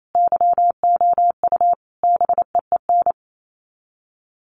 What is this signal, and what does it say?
Morse code, heard as a keyed tone, about 28 wpm, 700 Hz. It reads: YOU BEEN
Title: Sets of 5 Words Encore 28wpm